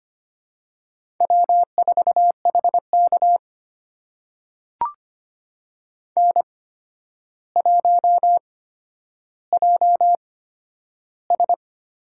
Morse code, heard as a keyed tone, about 25 wpm, spaced as for 11 wpm, 700 Hz. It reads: W4HK N1JS
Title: USA Call Signs - 8x Character Spacing - 25wpm